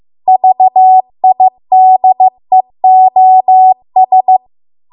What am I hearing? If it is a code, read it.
Morse code, heard as a keyed tone, about 15 wpm, 750 Hz. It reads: VIDEOS